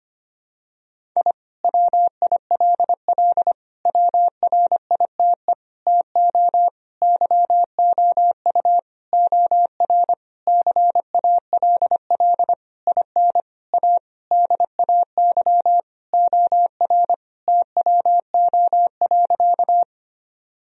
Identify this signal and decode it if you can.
Morse code, heard as a keyed tone, about 25 wpm, 700 Hz. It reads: I WILL WRITE TO YOU OR CALL IN A DAY OR TWO.
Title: Sentences from Top 1000 Words - Review Part 1 - 25wpm